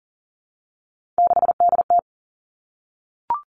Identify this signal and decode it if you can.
Morse code, heard as a keyed tone, about 38 wpm, 700 Hz. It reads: 6DT
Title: Three Challenging Letter-Number Combinations 40wpm